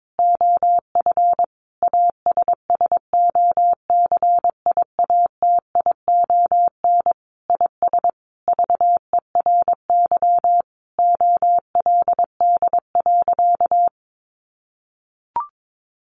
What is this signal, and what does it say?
Morse code, heard as a keyed tone, about 22 wpm, 700 Hz. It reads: OF ASSOCIATION IS VERY OLD.
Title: Sentences from Top 1100 Words 22wpm